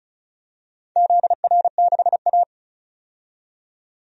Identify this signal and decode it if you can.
Morse code, heard as a keyed tone, about 36 wpm, 690 Hz. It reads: ZR6A
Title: DXCC-100 Call Signs - 4 Characters - 35wpm